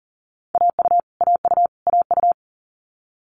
Morse code, heard as a keyed tone, about 40 wpm, 700 Hz. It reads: AU AU AU
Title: Multi-letter Phonemes Repeated 3x 40wpm